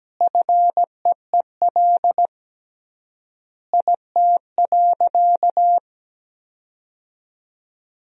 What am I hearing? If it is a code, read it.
Morse code, heard as a keyed tone, about 17 wpm, 700 Hz. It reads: FEEL IT.
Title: 3x Word-Spacing - Sentences from Top 400 Words - 17wpm